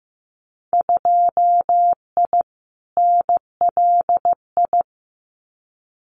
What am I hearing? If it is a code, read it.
Morse code, heard as a keyed tone, about 15 wpm, 700 Hz. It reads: 2I NLI
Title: Winter Field Day Exchanges 15wpm